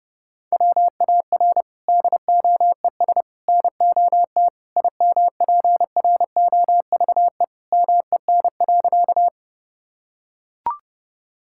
Morse code, heard as a keyed tone, about 30 wpm, 700 Hz. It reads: WAR DOES NOT IMPROVE MEN.
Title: Sentences from Top 1000 Words - Review Part 5 - 30wpm